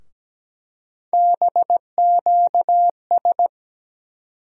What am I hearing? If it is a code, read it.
Morse code, heard as a keyed tone, about 17 wpm, 700 Hz. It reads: BQS
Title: USA Call Sign Postfixes - 3 Letters 17wpm